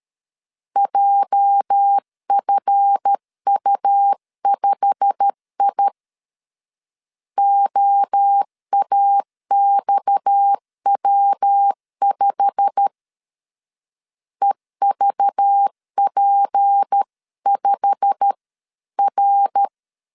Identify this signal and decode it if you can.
Morse code, heard as a keyed tone, about 13 wpm, 800 Hz. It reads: JFU5I OAXW5 EVP5R
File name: LEZIONI / 8 YJ45 / 8 YJ45 12wpm 2151